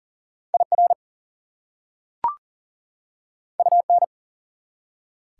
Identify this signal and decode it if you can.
Morse code, heard as a keyed tone, about 40 wpm, 700 Hz. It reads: IR UN
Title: Prefixes and Suffixes 40wpm